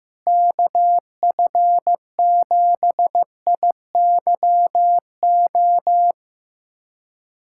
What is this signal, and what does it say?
Morse code, heard as a keyed tone, about 15 wpm, 700 Hz. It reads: KF7IYO
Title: USA Callsign DE Callsign - 6 Characters - 15wpm